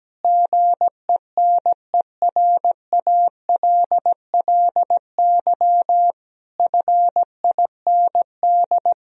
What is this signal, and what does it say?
Morse code, heard as a keyed tone, about 17 wpm, 700 Hz. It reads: GENERALLY FIND
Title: Sentences from Top 1000 Words - Review Part 4 - 17wpm